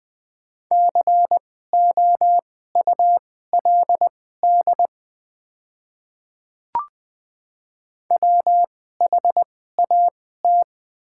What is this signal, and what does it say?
Morse code, heard as a keyed tone, about 20 wpm, spaced as for 15 wpm, 700 Hz. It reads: COULD WHAT
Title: 2x Character Spacing - Top 100 Words - 20wpm